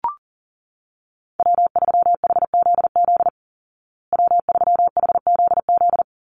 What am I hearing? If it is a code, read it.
Morse code, heard as a keyed tone, about 40 wpm, 700 Hz. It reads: W3HZZ W3HZZ
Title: Call Signs Repeated 3x - 40wpm